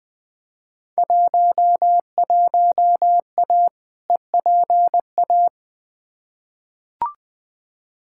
Morse code, heard as a keyed tone, about 20 wpm, 700 Hz. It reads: 11A EPA
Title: Field Day Exchanges Encore - 20wpm